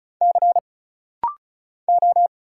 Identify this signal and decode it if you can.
Morse code, heard as a keyed tone, about 36 wpm, 700 Hz. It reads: C O
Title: Single Letter-Number Alt - Rapid-Fire 35wpm